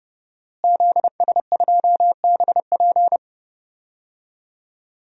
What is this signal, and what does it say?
Morse code, heard as a keyed tone, about 32 wpm, 700 Hz. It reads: ZS2BP
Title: DXCC-100 Call Signs - 5 Characters - Repeated 2x - 30wpm